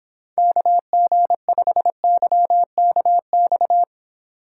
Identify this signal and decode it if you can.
Morse code, heard as a keyed tone, about 30 wpm, 700 Hz. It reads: KG5YKX